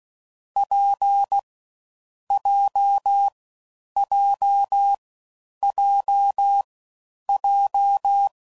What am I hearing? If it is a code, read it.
Morse code, heard as a keyed tone, about 16 wpm, 800 Hz. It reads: P J J J J
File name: Combinations / Difficult Combos / WPJ WPJ1